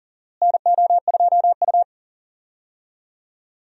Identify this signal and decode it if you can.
Morse code, heard as a keyed tone, about 37 wpm, 700 Hz. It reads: NO2U
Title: USA Call Signs - 4 Characters - Repeated 3x - 40wpm